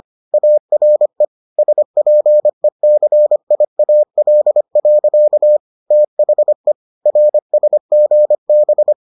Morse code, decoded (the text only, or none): ARE SPECIAL. THE RSGB